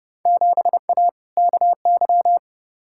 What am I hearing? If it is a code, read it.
Morse code, heard as a keyed tone, about 30 wpm, 700 Hz. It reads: 7A KY
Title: Field Day Exchanges Encore - 30wpm